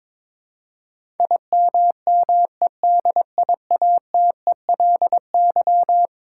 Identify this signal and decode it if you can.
Morse code, heard as a keyed tone, about 22 wpm, 700 Hz. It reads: IMMEDIATELY
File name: practice / 22wpm / Eleven Letter Words - 22wpm